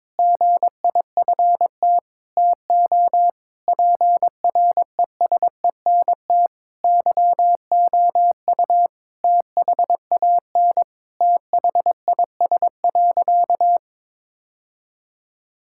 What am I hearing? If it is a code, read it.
Morse code, heard as a keyed tone, about 22 wpm, 700 Hz. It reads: GIFT TO PRESENT YOU THAN THIS.
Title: Sentences from Top 1000 Words - Review Part 6 - 22wpm